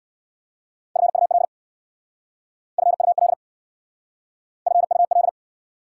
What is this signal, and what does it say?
Morse code, heard as a keyed tone, about 68 wpm, 700 Hz. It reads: FUL FUL FUL